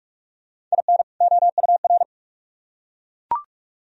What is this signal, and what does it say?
Morse code, heard as a keyed tone, about 44 wpm, 700 Hz. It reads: IN OUR